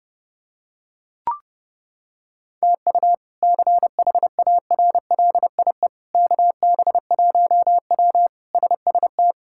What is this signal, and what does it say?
Morse code, heard as a keyed tone, about 30 wpm, 700 Hz. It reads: TU CHARLIE KB1W SST
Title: K1USN SST QSO Practice - 30wpm